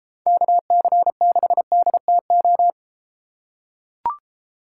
Morse code, heard as a keyed tone, about 32 wpm, 700 Hz. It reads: KC6DTO